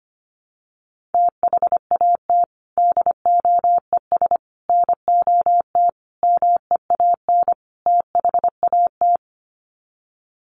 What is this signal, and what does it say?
Morse code, heard as a keyed tone, about 25 wpm, 700 Hz. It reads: THAT DOES NOT MEAN THAT
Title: Sets of 5 Words Encore 25wpm